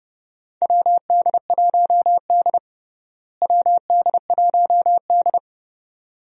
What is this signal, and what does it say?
Morse code, heard as a keyed tone, about 30 wpm, 700 Hz. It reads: WD1D WD1D
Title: USA Call Signs - 4 Characters - Repeated 2x - 30wpm